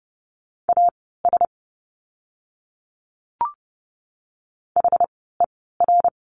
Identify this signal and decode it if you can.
Morse code, heard as a keyed tone, about 30 wpm, spaced as for 17 wpm, 700 Hz. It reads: AS HER
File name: course / 30:10wpm / 230 Lesson 37f - Letter x - Words 30wpm